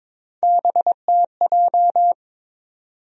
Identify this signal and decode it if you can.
Morse code, heard as a keyed tone, about 22 wpm, 700 Hz. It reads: BTJ
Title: USA Call Sign Postfixes - 3 Letters 22wpm